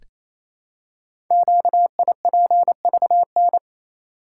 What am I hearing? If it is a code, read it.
Morse code, heard as a keyed tone, about 28 wpm, 700 Hz. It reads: QIPVN
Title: Code Groups - 5 Letter:Numbers - 28wpm